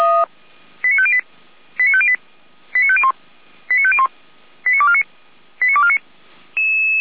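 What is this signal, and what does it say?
Five-tone selective call (ZVEI), heard as quick runs of five tones, tones of about 70 ms. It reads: 89508 89508 89501 89501 89249 89249
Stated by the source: Five-tone ZVEI selective call received by SDR.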